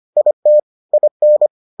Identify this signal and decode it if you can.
Morse code, heard as a keyed tone, about 25 wpm, 600 Hz. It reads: IT IN